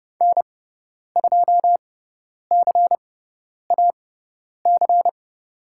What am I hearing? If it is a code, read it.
Morse code, heard as a keyed tone, about 30 wpm, spaced as for 10 wpm, 700 Hz. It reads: N2CAC
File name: Lesson 18g - Letter m - Call Signs 30wpm